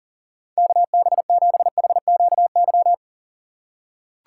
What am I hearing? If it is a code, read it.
Morse code, heard as a keyed tone, about 37 wpm, 700 Hz. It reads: KB7HQY
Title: USA Call Signs - 6 Characters - 4x Character Spacing - 40wpm